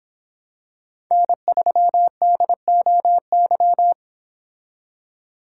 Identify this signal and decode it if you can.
Morse code, heard as a keyed tone, about 26 wpm, 700 Hz. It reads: N3DOY